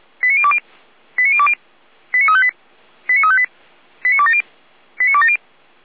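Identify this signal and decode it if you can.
Five-tone selective call (ZVEI), heard as quick runs of five tones, tones of about 70 ms. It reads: 89020 89020 89358 89358 89270 89270
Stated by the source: ZVEI five-tone call 89280 received by SDR.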